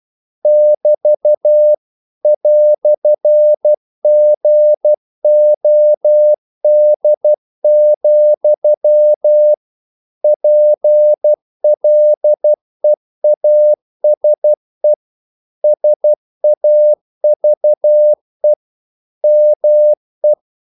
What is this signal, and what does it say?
Morse code, heard as a keyed tone, about 12 wpm, 600 Hz. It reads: <BT> "GOD, PLEASE SAVE ME
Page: English quotations in CW (telegraphy - morse)